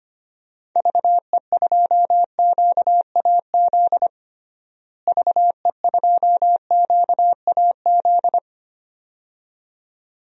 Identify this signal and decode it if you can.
Morse code, heard as a keyed tone, about 25 wpm, 700 Hz. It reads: VE2QAZ VE2QAZ
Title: DXCC-100 Call Signs - 6 Characters - Repeated 2x - 25wpm